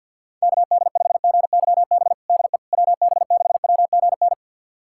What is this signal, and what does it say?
Morse code, heard as a keyed tone, about 47 wpm, 700 Hz. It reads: KD5GYB DE WB6WGN
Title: USA Callsign DE Callsign - 6 Characters - 50wpm